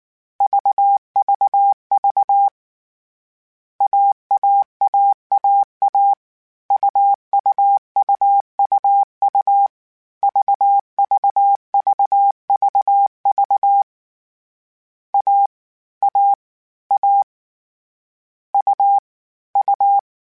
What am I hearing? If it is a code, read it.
Morse code, heard as a keyed tone, about 19 wpm, 800 Hz. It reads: VVV AAAAA UUUUU VVVVV A A A U U
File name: Difficult Combos / Difficult Combos-mp3s / AUV AUV4 @18